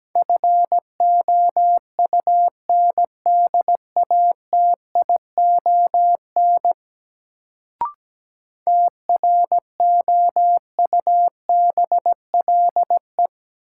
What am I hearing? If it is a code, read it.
Morse code, heard as a keyed tone, about 17 wpm, 700 Hz. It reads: FOUNDATION TROUBLE
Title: Sentences from Top 1200 Words - New Word Drill Alt - 17wpm